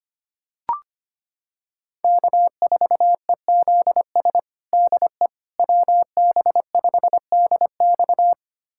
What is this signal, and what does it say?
Morse code, heard as a keyed tone, about 25 wpm, 700 Hz. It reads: K4EZS DE WB5DX
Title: USA Callsign DE Callsign - 5 Characters - 25wpm